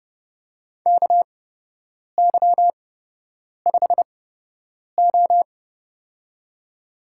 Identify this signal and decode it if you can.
Morse code, heard as a keyed tone, about 29 wpm, spaced as for 8 wpm, 700 Hz. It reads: KY5O
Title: USA Call Signs - 4 Characters - 8x Character Spacing - 30wpm